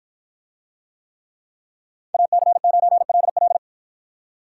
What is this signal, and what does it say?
Morse code, heard as a keyed tone, about 53 wpm, 700 Hz. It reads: AC9LL